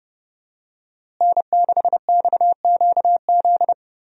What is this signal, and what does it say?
Morse code, heard as a keyed tone, about 32 wpm, 700 Hz. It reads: N6XQZ
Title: USA Call Signs - 5 Characters - Repeated 2x - 30wpm